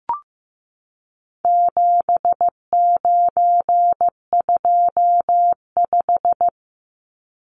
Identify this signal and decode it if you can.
Morse code, heard as a keyed tone, about 15 wpm, 700 Hz. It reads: 7925
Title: Numbers - 4-Digit 15wpm